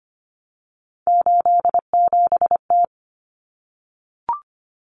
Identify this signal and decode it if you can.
Morse code, heard as a keyed tone, about 25 wpm, 700 Hz. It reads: 87T
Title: Common Cut Numbers - 3-Digit 25wpm